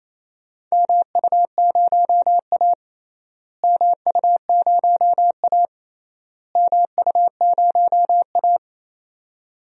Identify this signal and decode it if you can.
Morse code, heard as a keyed tone, about 28 wpm, 700 Hz. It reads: MU0A MU0A MU0A
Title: DXCC-100 Call Signs - 4 Characters - Repeated 3x - 28wpm